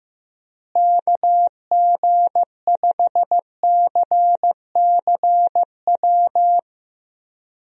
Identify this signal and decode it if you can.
Morse code, heard as a keyed tone, about 15 wpm, 700 Hz. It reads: KG5CCW
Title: USA Call Signs - 6 Characters - Repeated 3x - 15wpm